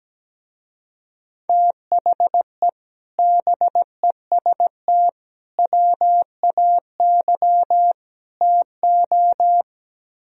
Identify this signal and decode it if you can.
Morse code, heard as a keyed tone, about 17 wpm, 700 Hz. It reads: THE BEST WAY TO
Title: Speed-Racing - Sets of 4 Words 17wpm